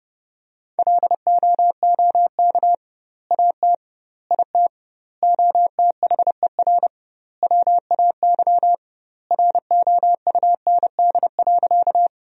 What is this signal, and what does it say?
Morse code, heard as a keyed tone, about 29 wpm, 700 Hz. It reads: LOOK AT IT OTHER WAY ROUND.
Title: 2x Word-Spacing - Sentences from Top 200 Words - 30wpm